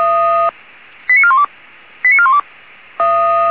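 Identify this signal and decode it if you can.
Five-tone selective call (ZVEI), heard as quick runs of five tones, tones of about 70 ms. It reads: 89412 89412
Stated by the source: Received on SDR as ZVEI five-tone selective call.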